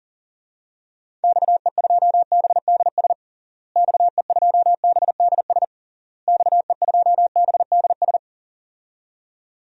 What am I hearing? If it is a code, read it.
Morse code, heard as a keyed tone, about 41 wpm, 700 Hz. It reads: XE2BDS XE2BDS XE2BDS
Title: DXCC-100 Call Signs - 6 Characters - Repeated 3x - 40wpm